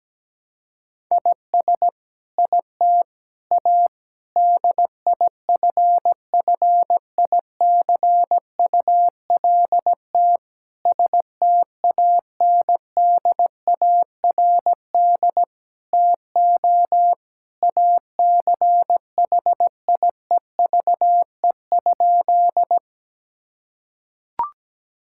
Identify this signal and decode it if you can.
Morse code, heard as a keyed tone, about 17 wpm, 700 Hz. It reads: IS IT A DIFFICULT STANDARD TO ACHIEVE?
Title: Sentences from Top 1000 Words - Review Part 6 - 17wpm